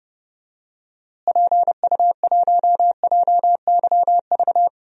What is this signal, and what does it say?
Morse code, heard as a keyed tone, about 29 wpm, 700 Hz. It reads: PU1JYV
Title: DXCC-100 Call Signs - 6 Characters - Repeated 2x - 30wpm